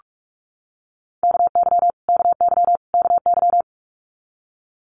Morse code, heard as a keyed tone, about 46 wpm, 700 Hz. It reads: KY KY KY
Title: Multi-letter Phonemes Repeated 3x 45wpm